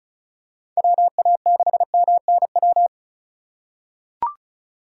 Morse code, heard as a keyed tone, about 36 wpm, 700 Hz. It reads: WA6MNW
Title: USA Call Signs - 6 Characters - 3x Character Spacing - 35wpm